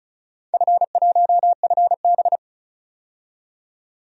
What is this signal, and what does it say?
Morse code, heard as a keyed tone, about 34 wpm, 700 Hz. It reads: F1FB
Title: DXCC-100 Call Signs - 4 Characters - 35wpm